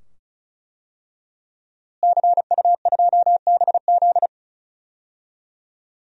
Morse code, heard as a keyed tone, about 36 wpm, 700 Hz. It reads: CU2BZ
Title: DXCC-100 Call Signs - 5 Characters - Repeated 2x - 35wpm